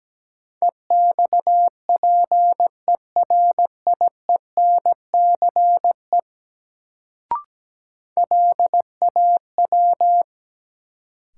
Morse code, heard as a keyed tone, about 17 wpm, 700 Hz. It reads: EXPERIENCE LAW
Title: Sentences from Top 600 Words - New Word Drill - 17wpm